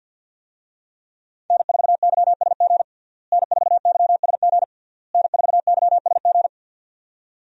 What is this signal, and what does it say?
Morse code, heard as a keyed tone, about 46 wpm, 700 Hz. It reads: N4YSG N4YSG N4YSG
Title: USA Call Signs - 5 Characters - Repeated 3x - 50wpm